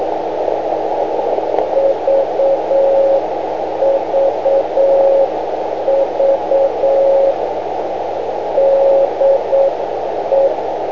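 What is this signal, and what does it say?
Morse code, heard as a keyed tone, about 8 wpm, 550 Hz, with radio noise. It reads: VVV DE
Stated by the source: Morse code from VX9MRC on 500 kc.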